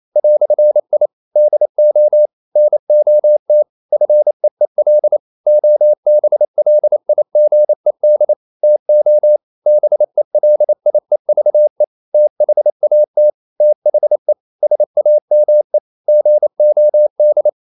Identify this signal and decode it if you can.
Morse code, heard as a keyed tone, about 28 wpm, 600 Hz. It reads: "I DO NOT FEEL OBLIGED TO BELIEVE THAT THE SAME GOD
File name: CW / Practice / EnglishQuotations0128